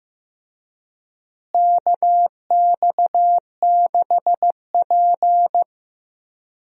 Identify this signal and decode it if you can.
Morse code, heard as a keyed tone, about 14 wpm, 700 Hz. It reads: KX6P
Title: USA Call Signs - 4 Characters - Repeated 3x - 15wpm